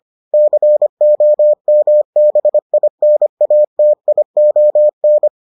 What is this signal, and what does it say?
Morse code, heard as a keyed tone, about 25 wpm, 600 Hz. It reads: COMBINATION